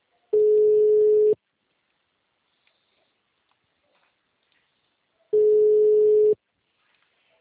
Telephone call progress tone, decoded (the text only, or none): ringback tone